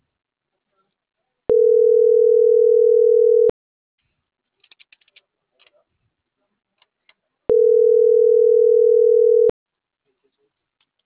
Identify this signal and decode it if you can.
Ringback tone (North America), 440 + 480 Hz.